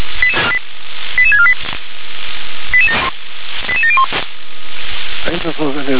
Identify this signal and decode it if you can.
Five-tone selective call (ZVEI), heard as quick runs of five tones, tones of about 70 ms, with speech. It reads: 88648 88701